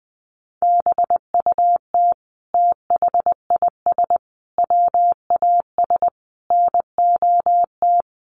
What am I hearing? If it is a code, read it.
Morse code, heard as a keyed tone, about 20 wpm, 700 Hz. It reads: BUT THIS WAS NOT
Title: Sentences from Top 600 Words 20wpm